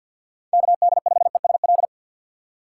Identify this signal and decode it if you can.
Morse code, heard as a keyed tone, about 52 wpm, 700 Hz. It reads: KD5ESL
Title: USA Call Signs - 3x Character Spacing - 50wpm